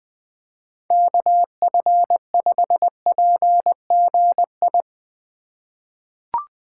Morse code, heard as a keyed tone, about 20 wpm, 700 Hz. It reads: KF5PGI
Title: USA Call Signs - 6 Characters - 2x Character Spacing - 20wpm